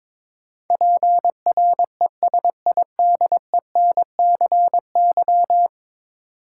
Morse code, heard as a keyed tone, about 22 wpm, 700 Hz. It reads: PRESIDENCY
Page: morse-code-ninja-practice-22wpm